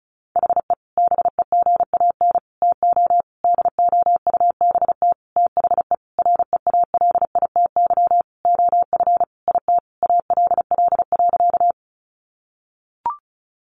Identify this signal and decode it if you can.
Morse code, heard as a keyed tone, about 34 wpm, 700 Hz. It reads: HE BEGAN TO DOUBT THE REALITY OF IT ALL.
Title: Sentences from Top 1500 Words 35wpm